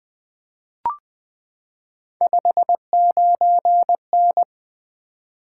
Morse code, heard as a keyed tone, about 20 wpm, 700 Hz. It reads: 59N
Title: Three Letter-Number Combinations - 20wpm